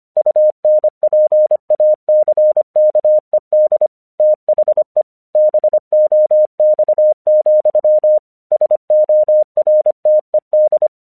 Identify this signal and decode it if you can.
Morse code, heard as a keyed tone, about 26 wpm, 610 Hz. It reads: UNPACKED THE BOX, SORTED
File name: 1 English 25wpm / AQC23